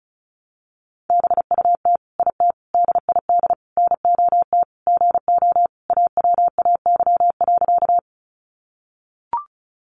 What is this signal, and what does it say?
Morse code, heard as a keyed tone, about 34 wpm, 700 Hz. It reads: BUT IT DID NOT GO AWAY.
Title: Sentences from Top 600 Words 35wpm